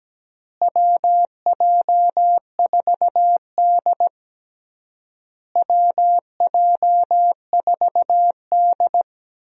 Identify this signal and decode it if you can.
Morse code, heard as a keyed tone, about 17 wpm, 700 Hz. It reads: WJ4D WJ4D
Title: USA Call Signs - 4 Characters - Repeated 3x - 17wpm